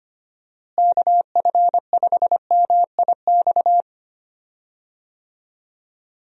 Morse code, heard as a keyed tone, about 25 wpm, 700 Hz. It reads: KF5MIX
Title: USA Call Signs - 6 Characters - Repeated 3x - 25wpm